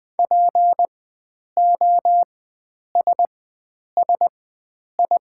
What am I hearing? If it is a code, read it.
Morse code, heard as a keyed tone, about 20 wpm, spaced as for 9 wpm, 700 Hz. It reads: POSSI
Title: 4x Character Spacing - Top 500 Words - 20wpm